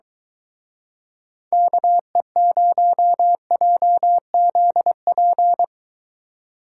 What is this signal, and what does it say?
Morse code, heard as a keyed tone, about 23 wpm, 700 Hz. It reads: KE0JZP